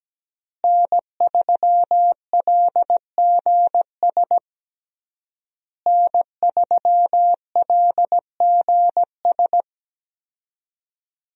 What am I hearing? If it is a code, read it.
Morse code, heard as a keyed tone, about 17 wpm, 700 Hz. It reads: N3LGS N3LGS
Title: USA Call Signs - 5 Characters - Repeated 3x - 17wpm